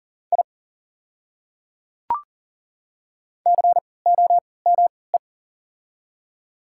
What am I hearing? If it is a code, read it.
Morse code, heard as a keyed tone, about 40 wpm, spaced as for 23 wpm, 690 Hz. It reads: I COME
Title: 3x Character Spacing - Top 100 Words - 40wpm